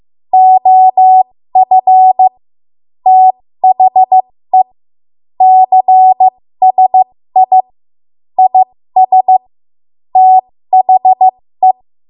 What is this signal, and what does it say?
Morse code, heard as a keyed tone, about 15 wpm, spaced as for 13 wpm, 750 Hz. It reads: OF THE CSI IS THE
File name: Morse / 13WPM